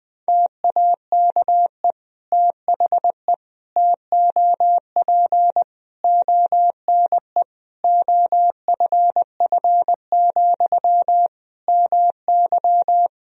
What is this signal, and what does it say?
Morse code, heard as a keyed tone, about 21 wpm, 700 Hz. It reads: TAKE THE TOP ONE OFF, MY
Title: Sentences from Top 1000 Words - Review Part 2 - 20wpm